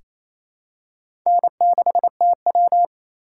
Morse code, heard as a keyed tone, about 28 wpm, 700 Hz. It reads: N6TW
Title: K1USN SST QSO Practice - 28wpm